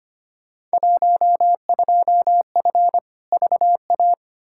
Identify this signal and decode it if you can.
Morse code, heard as a keyed tone, about 25 wpm, 700 Hz. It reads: 12F VA
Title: Field Day Exchanges Encore - 25wpm